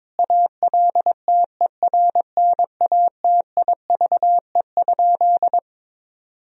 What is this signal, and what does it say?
Morse code, heard as a keyed tone, about 22 wpm, 700 Hz. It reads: ALTERNATIVE?
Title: Sentences from Top 1000 Words - Review Part 4 - 22wpm